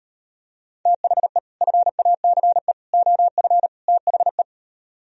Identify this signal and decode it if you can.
Morse code, heard as a keyed tone, about 38 wpm, 700 Hz. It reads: THE FACE OF THE